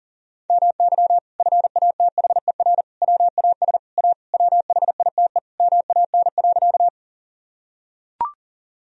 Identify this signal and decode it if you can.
Morse code, heard as a keyed tone, about 37 wpm, 700 Hz. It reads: MY FATHER WAS A WHITE MAN.
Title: Sentences from Top 1000 Words - Review Part 1 - 40wpm